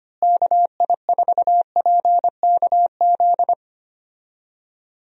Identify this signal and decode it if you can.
Morse code, heard as a keyed tone, about 25 wpm, 700 Hz. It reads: KI4PKZ
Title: USA Call Signs - 6 Characters - Repeated 3x - 25wpm